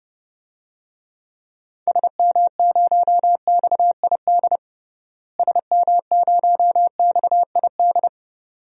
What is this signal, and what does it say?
Morse code, heard as a keyed tone, about 32 wpm, 700 Hz. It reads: SM0XID SM0XID
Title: DXCC-100 Call Signs - 6 Characters - Repeated 2x - 30wpm